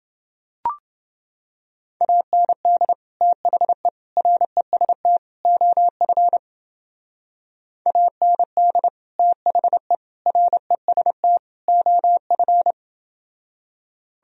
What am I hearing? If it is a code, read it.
Morse code, heard as a keyed tone, about 29 wpm, 700 Hz. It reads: AND THE REST OF AND THE REST OF